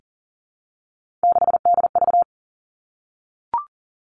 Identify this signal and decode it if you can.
Morse code, heard as a keyed tone, about 40 wpm, 700 Hz. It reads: 6DV
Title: All Cut Numbers - 3-Digit 40wpm